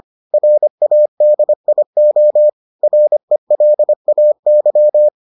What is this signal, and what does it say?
Morse code, heard as a keyed tone, about 25 wpm, 600 Hz. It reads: RADIO RELAY